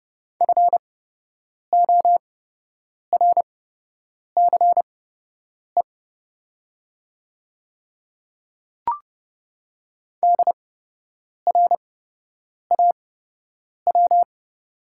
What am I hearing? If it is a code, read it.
Morse code, heard as a keyed tone, about 28 wpm, spaced as for 8 wpm, 700 Hz. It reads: FORCE DRAW
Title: 8x Character Spacing - Top 500 Words - 30wpm